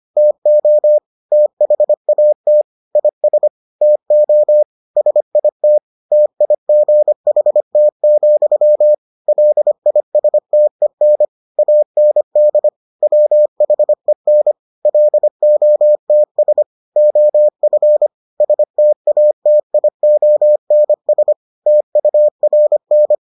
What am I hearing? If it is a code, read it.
Morse code, heard as a keyed tone, about 25 wpm, 600 Hz. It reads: TO THAT IS TO SIT TIGHT, LISTEN AND WHEN LOTS OF STATIONS TURN